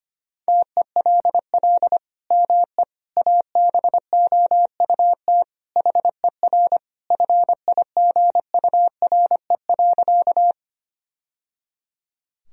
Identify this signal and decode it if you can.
Morse code, heard as a keyed tone, about 25 wpm, 700 Hz. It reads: TELL ME ABOUT HER FIGURE.
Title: Sentences from Top 1000 Words - Review Part 2 - 25wpm